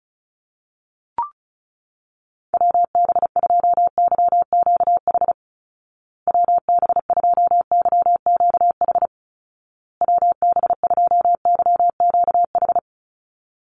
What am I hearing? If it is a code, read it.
Morse code, heard as a keyed tone, about 34 wpm, 700 Hz. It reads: WB2YQH WB2YQH WB2YQH
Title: Call Signs Repeated 2x - 35wpm